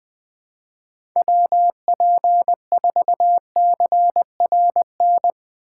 Morse code, heard as a keyed tone, about 20 wpm, 700 Hz. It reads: WP4CRN
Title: USA Call Signs - 6 Characters - 2x Character Spacing - 20wpm